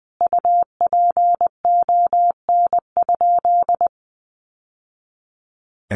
Morse code, heard as a keyed tone, about 20 wpm, 700 Hz. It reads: UPON?
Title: Sentences from Top 1100 Words 20wpm